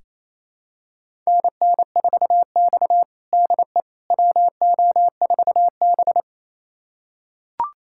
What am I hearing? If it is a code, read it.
Morse code, heard as a keyed tone, about 28 wpm, 700 Hz. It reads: NN4X DE WO4B
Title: USA Callsign DE Callsign - 4 Characters - 28wpm